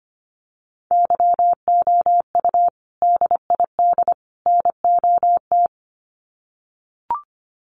Morse code, heard as a keyed tone, about 25 wpm, 700 Hz. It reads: YOU DID NOT
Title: Sets of 3 Words 25wpm